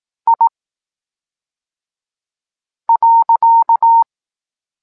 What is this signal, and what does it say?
Morse code, heard as a keyed tone, about 18 wpm, 920 Hz. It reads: I .
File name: Tekst-26 1